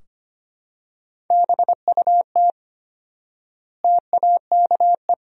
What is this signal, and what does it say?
Morse code, heard as a keyed tone, about 25 wpm, 700 Hz. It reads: BUT TAKE
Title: 4x Word-Spacing - Sentences from Top 300 Words - 25wpm